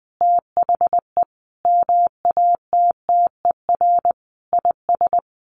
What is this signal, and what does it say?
Morse code, heard as a keyed tone, about 20 wpm, 700 Hz. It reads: THE MATTER IS
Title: Sets of 5 Words Encore2 20wpm